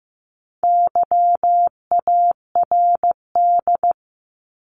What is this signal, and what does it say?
Morse code, heard as a keyed tone, about 15 wpm, 700 Hz. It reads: YARD
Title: Top 1751-2000 Words 15wpm